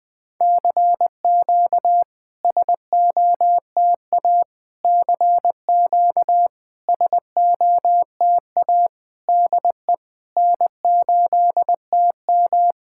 Morse code, heard as a keyed tone, about 20 wpm, 690 Hz. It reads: CQ SOTA CQ SOTA DE N8TM
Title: SOTA QSO Practice 20wpm